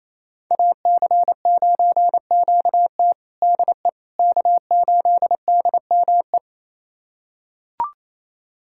Morse code, heard as a keyed tone, about 28 wpm, 700 Hz. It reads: AC9QT DE K8DME
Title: USA Callsign DE Callsign - 5 Characters - 28wpm